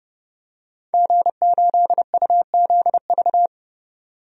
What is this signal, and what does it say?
Morse code, heard as a keyed tone, about 29 wpm, 700 Hz. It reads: G8UZV
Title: DXCC-100 Call Signs - 5 Characters - 30wpm